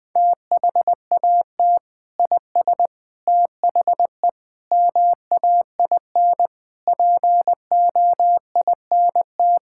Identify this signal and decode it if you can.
Morse code, heard as a keyed tone, about 20 wpm, 700 Hz. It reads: THAT IS THE MAIN POINT
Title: Sentences from Top 1000 Words - Review Part 5 - 20wpm